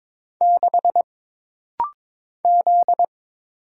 Morse code, heard as a keyed tone, about 22 wpm, 700 Hz. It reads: 6 Z
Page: This rapid-fire timing is geared to push your ICR (Instant Character Recognition) skills.